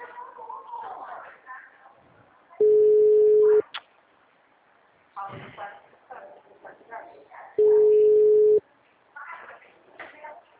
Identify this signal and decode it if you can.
Ringback tone (Europe, 425 Hz), with speech.